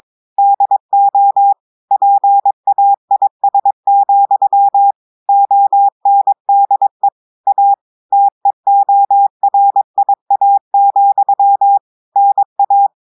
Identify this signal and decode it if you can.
Morse code, heard as a keyed tone, about 22 wpm, 800 Hz. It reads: DO PAIS, ONDE A TEORIA, NA
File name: audios / portugais / portugais texte en clair 1 vitesse 22 mots minutes